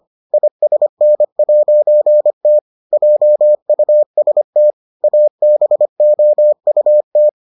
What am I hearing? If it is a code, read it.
Morse code, heard as a keyed tone, about 25 wpm, 600 Hz. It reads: ISN'T JUST ABOUT